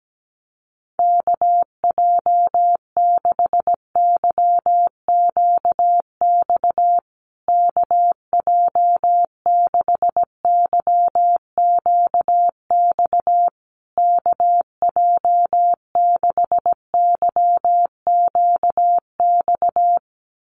Morse code, heard as a keyed tone, about 17 wpm, 700 Hz. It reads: KJ6YQX KJ6YQX KJ6YQX
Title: USA Call Signs Repeated 3x 17wpm